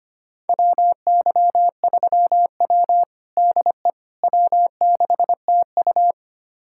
Morse code, heard as a keyed tone, about 25 wpm, 700 Hz. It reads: WY3W DE W6TU
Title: USA Callsign DE Callsign - 4 Characters - 25wpm